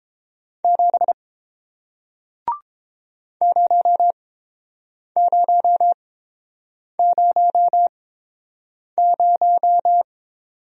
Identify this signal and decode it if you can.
Morse code, heard as a keyed tone, about 30 wpm, spaced as for 7 wpm, 700 Hz. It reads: 7 0000